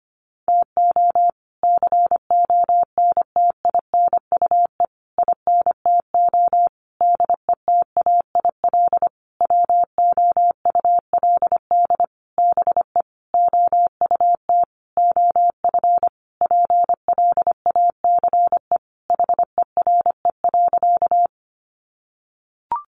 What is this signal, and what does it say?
Morse code, heard as a keyed tone, about 25 wpm, 700 Hz. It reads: TO CONTINUE INTO DETAIL WOULD BE OUT OF PLACE HERE.
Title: Sentences from Top 1500 Words 25wpm